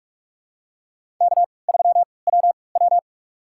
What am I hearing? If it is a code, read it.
Morse code, heard as a keyed tone, about 46 wpm, spaced as for 26 wpm, 700 Hz. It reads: K3WW
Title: USA Call Signs - 3x Character Spacing - 45wpm